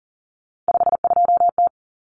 Morse code, heard as a keyed tone, about 40 wpm, 700 Hz. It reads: 52T
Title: Common Cut Numbers - 3-Digit 40wpm